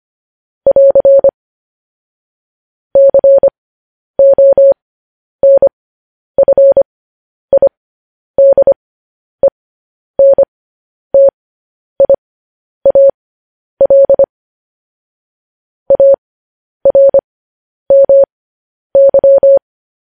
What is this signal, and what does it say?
Morse code, heard as a keyed tone, about 25 wpm, spaced as for 10 wpm, 550 Hz. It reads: <AR> CONFIDENTIAL ARMY
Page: news-headlines-in-morse-code-at-10-wpm